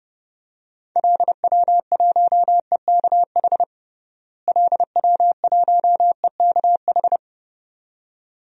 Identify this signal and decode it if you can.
Morse code, heard as a keyed tone, about 32 wpm, 700 Hz. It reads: LW1EKH LW1EKH
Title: DXCC-100 Call Signs - 6 Characters - Repeated 2x - 30wpm